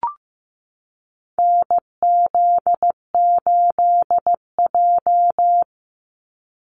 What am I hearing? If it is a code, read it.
Morse code, heard as a keyed tone, about 15 wpm, 700 Hz. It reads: NZ8J
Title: Call Signs Repeated 2x - 15wpm